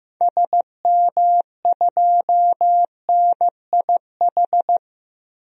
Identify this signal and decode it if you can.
Morse code, heard as a keyed tone, about 15 wpm, 700 Hz. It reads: SM2NIH
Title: DXCC-100 Call Signs - 6 Characters - 15wpm